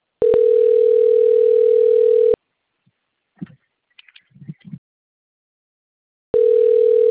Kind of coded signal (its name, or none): telephone call progress tone